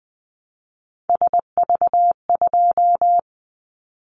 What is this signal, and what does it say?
Morse code, heard as a keyed tone, about 20 wpm, 700 Hz. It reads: SV2
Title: Callsign Prefixes for Top 50 DXCC - 20wpm